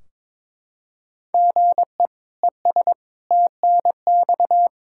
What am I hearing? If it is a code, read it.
Morse code, heard as a keyed tone, about 21 wpm, 700 Hz. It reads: GE ES TNX
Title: Practice QSO - Exchange 1 - Beginning - 22wpm